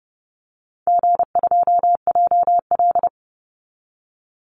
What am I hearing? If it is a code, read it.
Morse code, heard as a keyed tone, about 30 wpm, 700 Hz. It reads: G2JL
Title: Easy Call Signs Repeated 2x - 30wpm